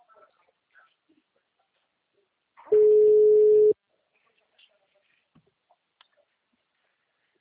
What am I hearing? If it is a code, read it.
Ringback tone (Europe, 425 Hz).
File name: out- -2010- - - .22